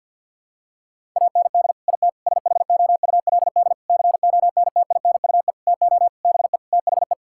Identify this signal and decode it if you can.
Morse code, heard as a keyed tone, about 47 wpm, 700 Hz. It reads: AND IT SHOULD CONTINUE TO BE THE
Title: Sentences from Top 1000 Words - Review Part 6 - 50wpm